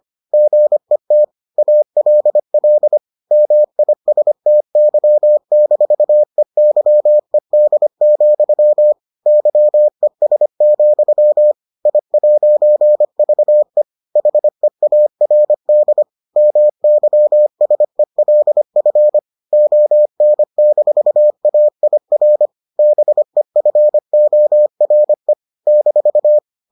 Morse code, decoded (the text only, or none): GET ALL MISTY-EYED, YES, I'VE HEARD MYSELF ON-AIR BEFORE -